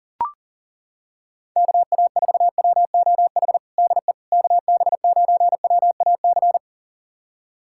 Morse code, heard as a keyed tone, about 37 wpm, 700 Hz. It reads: KA4WOH DE KB9WAC
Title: USA Callsign DE Callsign 40wpm